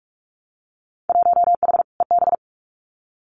Morse code, heard as a keyed tone, about 46 wpm, 700 Hz. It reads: 1H EB
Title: Winter Field Day Exchanges 45wpm